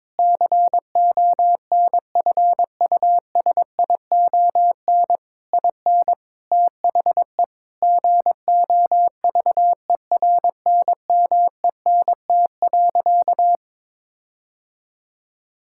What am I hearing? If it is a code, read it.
Morse code, heard as a keyed tone, about 22 wpm, 700 Hz. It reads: CONFUSION IN THE GOVERNMENT.
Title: Sentences from Top 1600 Words 22wpm